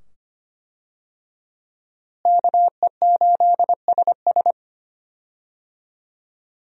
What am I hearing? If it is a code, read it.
Morse code, heard as a keyed tone, about 25 wpm, 700 Hz. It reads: KE8SS
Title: USA Call Signs - 5 Characters - Repeated 2x - 25wpm